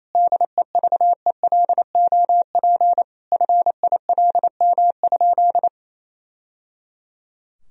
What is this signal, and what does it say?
Morse code, heard as a keyed tone, about 28 wpm, 700 Hz. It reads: DEVELOP FILM?
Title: Sentences from Top 1000 Words - Review Part 3 - 28wpm